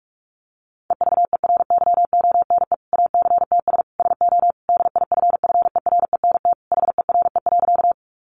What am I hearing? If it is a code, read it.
Morse code, heard as a keyed tone, about 44 wpm, 700 Hz. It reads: EVERYONE ACTS SO DIFFERENT HERE.
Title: Easy Phrases 45wpm